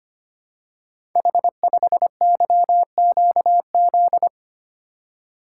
Morse code, heard as a keyed tone, about 25 wpm, 700 Hz. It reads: H5YQZ
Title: Code Groups - 5 Letter:Numbers - 25wpm